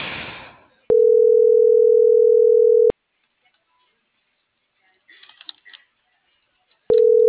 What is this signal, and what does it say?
Ringback tone (North America), 440 + 480 Hz.